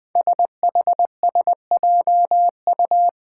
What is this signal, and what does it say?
Morse code, heard as a keyed tone, about 20 wpm, 700 Hz. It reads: SHSJU
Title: Five Challenging Letter-Number Combinations 20wpm